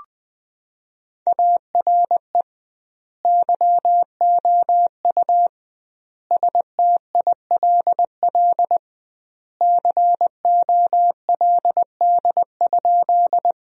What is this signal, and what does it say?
Morse code, heard as a keyed tone, about 20 wpm, 700 Hz. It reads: ARE YOU STILL COLD?
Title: 2x Word-Spacing - Sentences from Top 400 Words - 20wpm